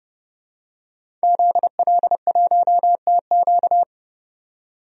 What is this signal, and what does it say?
Morse code, heard as a keyed tone, about 30 wpm, 700 Hz. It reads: ZL1TQ
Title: DXCC-100 Call Signs - 5 Characters - 30wpm